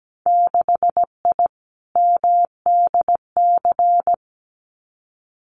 Morse code, heard as a keyed tone, about 17 wpm, 700 Hz. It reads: 6I MDC
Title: Winter Field Day Exchanges Encore 17wpm